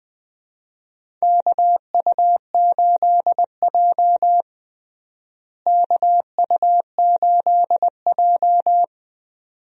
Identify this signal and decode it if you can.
Morse code, heard as a keyed tone, about 20 wpm, 700 Hz. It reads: KU8J KU8J
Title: USA Call Signs - 4 Characters - Repeated 2x - 20wpm